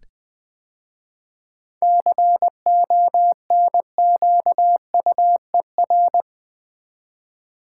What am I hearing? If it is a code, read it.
Morse code, heard as a keyed tone, about 20 wpm, 700 Hz. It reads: CONQUER
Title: 7 Letter Words - First 3 Letters Given - 20wpm